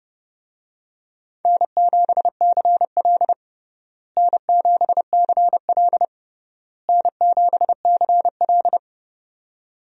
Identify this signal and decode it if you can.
Morse code, heard as a keyed tone, about 30 wpm, 700 Hz. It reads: N7CL N7CL N7CL
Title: USA Call Signs - 4 Characters - Repeated 3x - 30wpm